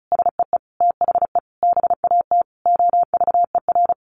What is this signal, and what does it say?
Morse code, heard as a keyed tone, about 34 wpm, 700 Hz. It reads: SEE THE BAT OVER
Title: Sentences from Top 1500 Words 35wpm